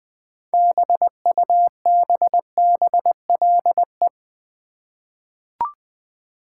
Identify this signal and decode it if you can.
Morse code, heard as a keyed tone, about 20 wpm, 700 Hz. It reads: BUBBLE
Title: 6 Letter Words - First 3 Letters Given - 20wpm